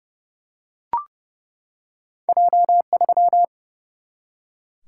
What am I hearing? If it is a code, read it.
Morse code, heard as a keyed tone, about 30 wpm, 700 Hz. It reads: J3
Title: Single-Letter Single-Number 30wpm